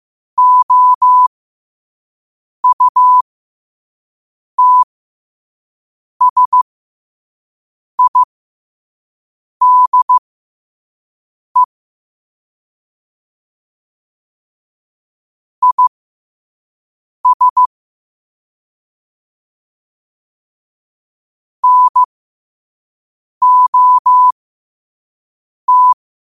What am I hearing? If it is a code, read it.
Morse code, heard as a keyed tone, about 15 wpm, spaced as for 5 wpm, 1000 Hz. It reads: OUTSIDE IS NOT
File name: qotdfiles / 5wpm